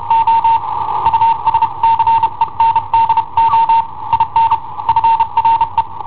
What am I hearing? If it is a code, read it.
Morse code, heard as a keyed tone, about 32 wpm, 910 Hz. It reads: O USCENDO IN FRE
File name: i7ohp / cassetta i7ohp B